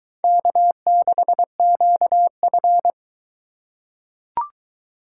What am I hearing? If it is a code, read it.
Morse code, heard as a keyed tone, about 22 wpm, 700 Hz. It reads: K6QF